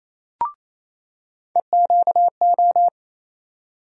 Morse code, heard as a keyed tone, about 28 wpm, 700 Hz. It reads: EQO
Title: USA Call Sign Postfixes - 3 Letters 28wpm